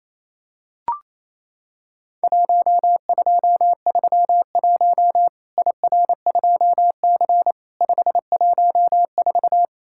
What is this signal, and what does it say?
Morse code, heard as a keyed tone, about 28 wpm, 700 Hz. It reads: 1231 IR2C 514
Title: Worked All Europe DX Contest - QTC Exchange - 28wpm